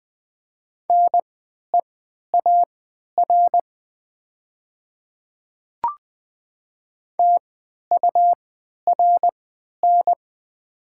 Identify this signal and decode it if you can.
Morse code, heard as a keyed tone, about 20 wpm, spaced as for 11 wpm, 700 Hz. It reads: NEAR TURN
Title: 3x Character Spacing - Top 200 Words - 20wpm